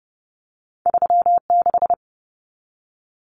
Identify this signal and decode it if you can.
Morse code, heard as a keyed tone, about 30 wpm, 700 Hz. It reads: 36
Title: Common Cut Numbers - 2-Digit 30wpm